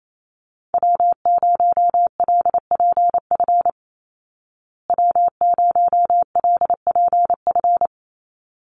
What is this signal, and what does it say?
Morse code, heard as a keyed tone, about 28 wpm, 700 Hz. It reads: W0LPF W0LPF
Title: Call Signs Repeated 3x - 28wpm